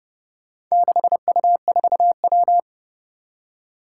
Morse code, heard as a keyed tone, about 28 wpm, 690 Hz. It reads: 6U4W
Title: Four Challenging Letter-Number Combinations 30wpm